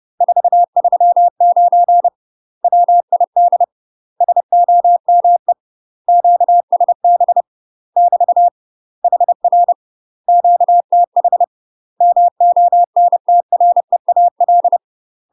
Morse code, heard as a keyed tone, about 30 wpm, 700 Hz. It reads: 439 WID SOME QSB <BT> HR QTH MONTREAL
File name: UK MASTER QSOs TRAINING/30WPM